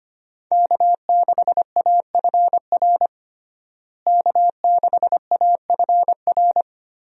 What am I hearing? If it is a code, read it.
Morse code, heard as a keyed tone, about 25 wpm, 700 Hz. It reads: K6AFR K6AFR
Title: USA Call Signs - 5 Characters - Repeated 3x - 25wpm